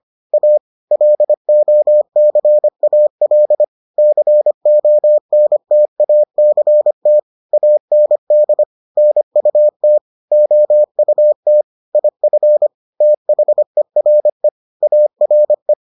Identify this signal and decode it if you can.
Morse code, heard as a keyed tone, about 25 wpm, 600 Hz. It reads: A LOCAL CONTACT AND NUT OUT IF THERE ARE